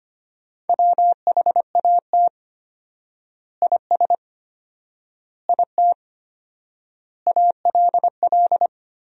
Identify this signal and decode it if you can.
Morse code, heard as a keyed tone, about 25 wpm, 700 Hz. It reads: WHAT IS IT ALL
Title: 4x Word-Spacing - Sentences from Top 100 Words - 25wpm